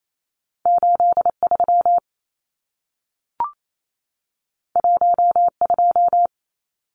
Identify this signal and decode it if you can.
Morse code, heard as a keyed tone, about 28 wpm, 700 Hz. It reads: 83 12
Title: Numbers - 2-Digit 28wpm